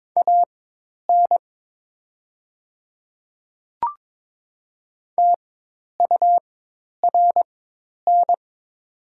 Morse code, heard as a keyed tone, about 21 wpm, spaced as for 10 wpm, 700 Hz. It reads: AN TURN
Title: 4x Character Spacing - Top 200 Words - 22wpm